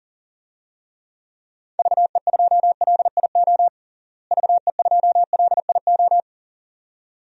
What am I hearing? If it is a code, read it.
Morse code, heard as a keyed tone, about 40 wpm, 700 Hz. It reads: VE2LIO VE2LIO
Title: DXCC-100 Call Signs - 6 Characters - Repeated 2x - 40wpm